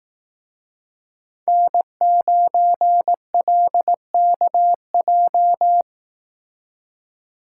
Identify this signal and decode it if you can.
Morse code, heard as a keyed tone, about 18 wpm, 700 Hz. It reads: N9LKJ
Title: Speed Racing - USA 1x3 Call Signs 15wpm